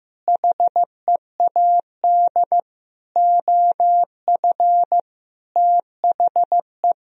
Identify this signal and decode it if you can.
Morse code, heard as a keyed tone, about 15 wpm, 700 Hz. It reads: HEAD OF THE
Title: Sentences from Top 1000 Words - Review Part 4 - 15wpm